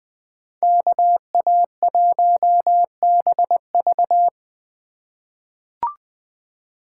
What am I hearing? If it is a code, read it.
Morse code, heard as a keyed tone, about 21 wpm, 700 Hz. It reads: KA1BV
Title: USA Call Signs - 5 Characters - 2x Character Spacing - 20wpm